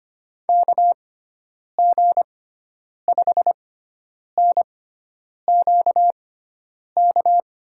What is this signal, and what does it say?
Morse code, heard as a keyed tone, about 25 wpm, spaced as for 9 wpm, 700 Hz. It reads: KG5NQK
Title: USA Call Signs - 6 Characters - 6x Character Spacing - 25wpm